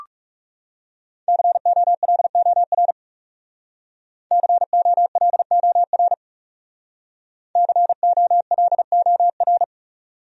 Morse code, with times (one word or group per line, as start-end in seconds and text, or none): 1.3-2.9 COLOR
4.3-6.1 COLOR
7.6-9.6 COLOR